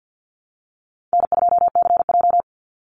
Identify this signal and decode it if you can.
Morse code, heard as a keyed tone, about 50 wpm, 700 Hz. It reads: N2CJ
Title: Call Signs - 50wpm